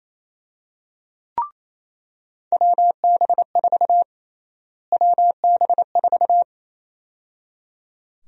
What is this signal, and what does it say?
Morse code, heard as a keyed tone, about 27 wpm, 700 Hz. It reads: WB4 WB4
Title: USA Call Sign Prefixes - Repeated 2x - 28wpm